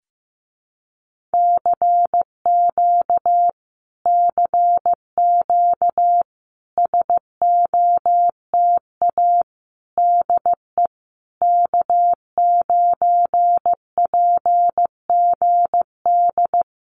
Morse code, heard as a keyed tone, about 15 wpm, 700 Hz. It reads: CQ CQ SOTA DE K9PGD
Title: Common QSO Phrases Encore 15wpm